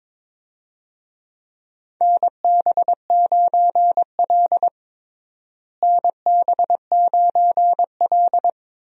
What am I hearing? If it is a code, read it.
Morse code, heard as a keyed tone, about 22 wpm, 700 Hz. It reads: NB9L NB9L
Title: USA Call Signs - 4 Characters - Repeated 2x - 22wpm